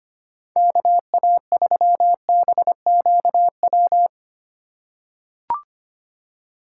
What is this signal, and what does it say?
Morse code, heard as a keyed tone, about 25 wpm, 700 Hz. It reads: KA3BQW
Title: USA Call Signs - 6 Characters - 3x Character Spacing - 25wpm